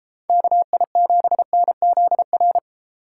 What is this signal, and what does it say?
Morse code, heard as a keyed tone, about 33 wpm, 700 Hz. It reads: KI7NZR